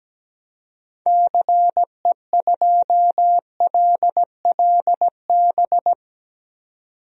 Morse code, heard as a keyed tone, about 17 wpm, 700 Hz. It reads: CE2LLB
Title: DXCC-100 Call Signs - 6 Characters - 17wpm